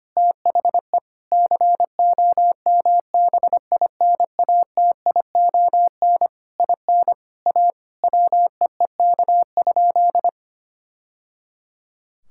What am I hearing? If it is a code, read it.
Morse code, heard as a keyed tone, about 25 wpm, 700 Hz. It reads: THE COMBINATION IN A WEEK?
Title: Sentences from Top 1000 Words - Review Part 6 - 25wpm